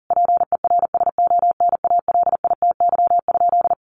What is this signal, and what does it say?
Morse code, heard as a keyed tone, about 37 wpm, 700 Hz. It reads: PERSONALITY?
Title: Sentences from Top 1400 Words 40wpm